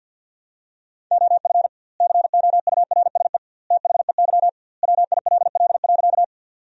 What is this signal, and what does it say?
Morse code, heard as a keyed tone, about 53 wpm, 700 Hz. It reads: OF COURSE THEY WILL.
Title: 2x Word-Spacing - Sentences from Top 500 Words - 50wpm